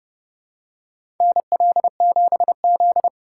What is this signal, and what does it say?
Morse code, heard as a keyed tone, about 30 wpm, 700 Hz. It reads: NL7Z
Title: USA Call Signs - 4 Characters - 2x Character Spacing - 30wpm